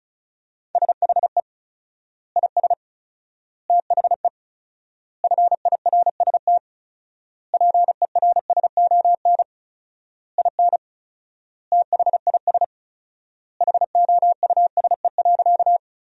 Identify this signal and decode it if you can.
Morse code, heard as a keyed tone, about 34 wpm, 700 Hz. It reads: SHE IS THE FIRST PERSON IN THIS HOUSE.
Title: 4x Word-Spacing - Sentences from Top 400 Words - 35wpm